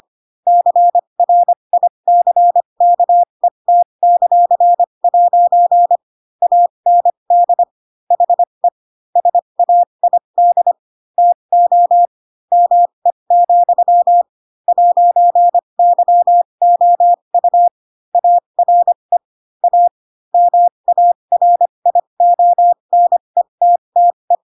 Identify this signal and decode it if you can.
Morse code, heard as a keyed tone, about 25 wpm, spaced as for 22 wpm, 700 Hz. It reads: CRICKET;' AND HE SAID TO ME, 'YOU ARE A MARIONETTE
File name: Pinocchio / Pinocchio ch7 18 25 ik1hge